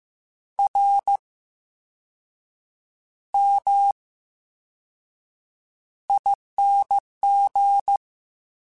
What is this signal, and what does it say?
Morse code, heard as a keyed tone, about 15 wpm, spaced as for 6 wpm, 800 Hz. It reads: RMING
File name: Special INGs / Special INGs-mp3s / Slow ing @13